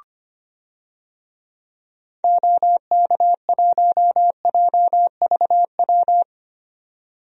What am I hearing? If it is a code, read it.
Morse code, heard as a keyed tone, about 25 wpm, 700 Hz. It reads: OK1JVW
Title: DXCC-100 Call Signs - 6 Characters - 25wpm